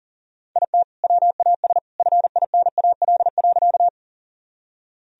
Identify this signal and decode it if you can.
Morse code, heard as a keyed tone, about 40 wpm, 700 Hz. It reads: IT WAS FINAL.
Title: Sentences from Top 1000 Words - Review Part 2 - 40wpm